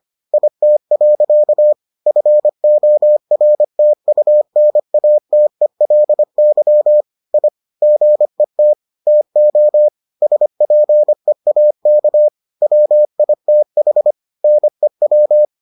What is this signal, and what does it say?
Morse code, heard as a keyed tone, about 25 wpm, 600 Hz. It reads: IT. FORTUNATELY I GET TO SPEAK WITH NEW